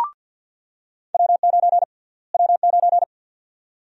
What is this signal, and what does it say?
Morse code, heard as a keyed tone, about 49 wpm, 700 Hz. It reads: W9 W9
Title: USA Call Sign Prefixes - Repeated 2x - 50wpm